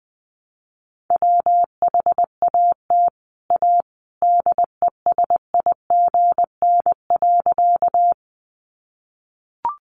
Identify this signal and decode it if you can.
Morse code, heard as a keyed tone, about 20 wpm, 700 Hz. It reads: WHAT A DESIGN.
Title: Sentences from Top 800 Words 20wpm